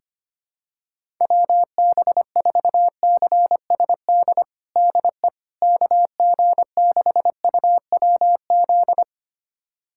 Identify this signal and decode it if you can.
Morse code, heard as a keyed tone, about 25 wpm, 700 Hz. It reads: WB4CSD DE KG6UWZ
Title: USA Callsign DE Callsign 25wpm